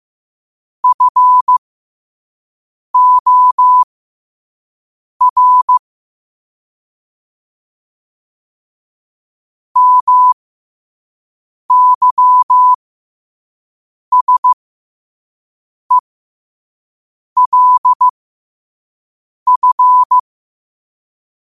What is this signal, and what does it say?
Morse code, heard as a keyed tone, about 15 wpm, spaced as for 5 wpm, 1000 Hz. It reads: FOR MYSELF
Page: New quotes every day in morse code at 5 Words per minute.